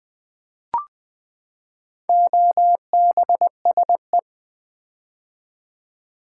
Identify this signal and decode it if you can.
Morse code, heard as a keyed tone, about 20 wpm, 700 Hz. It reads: OBSE
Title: Build A Word - 4 Letters At A Time - 11-Letter Words - 20wpm